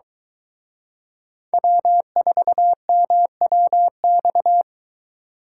Morse code, heard as a keyed tone, about 23 wpm, 700 Hz. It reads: W4MWX